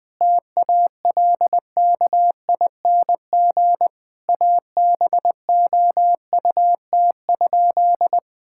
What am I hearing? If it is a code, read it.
Morse code, heard as a keyed tone, about 20 wpm, 700 Hz. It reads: TALKING ABOUT?
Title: Sentences with 2-Word Phrases - 20wpm